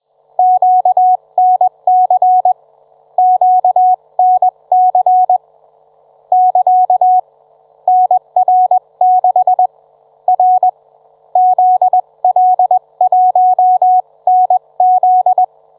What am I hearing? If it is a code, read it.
Morse code, heard as a keyed tone, about 21 wpm, 740 Hz. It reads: QNC QNC <KA> NR6 R ZL1NZ